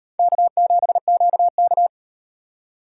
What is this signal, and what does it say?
Morse code, heard as a keyed tone, about 36 wpm, 700 Hz. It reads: K7QK